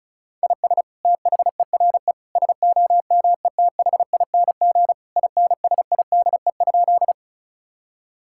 Morse code, heard as a keyed tone, about 34 wpm, 690 Hz. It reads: IS THERE SOMETHING INSIDE?
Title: Sentences from Top 1000 Words - Review Part 6 - 35wpm